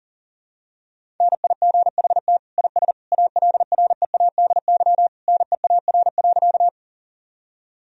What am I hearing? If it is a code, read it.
Morse code, heard as a keyed tone, about 40 wpm, 700 Hz. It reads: NIGHT IS ALREADY NEAR.
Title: Sentences from Top 1000 Words - Review Part 1 - 40wpm